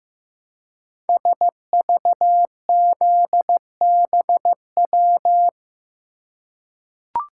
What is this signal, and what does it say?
Morse code, heard as a keyed tone, about 15 wpm, 700 Hz. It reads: SVZBW
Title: Code Groups - 5 Letters - 15wpm